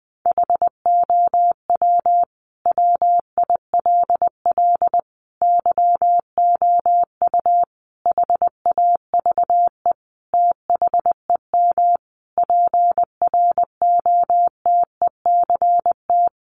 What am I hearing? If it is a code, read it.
Morse code, heard as a keyed tone, about 20 wpm, 700 Hz. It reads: HOW WILL YOU HAVE THEM PROTECT
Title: Sentences from Top 900 Words 20wpm